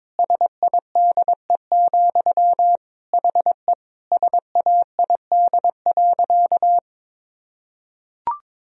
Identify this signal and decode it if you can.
Morse code, heard as a keyed tone, about 22 wpm, 700 Hz. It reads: SIDE, HE SAID.
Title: Sentences from Top 1000 Words - Review Part 1 - 22wpm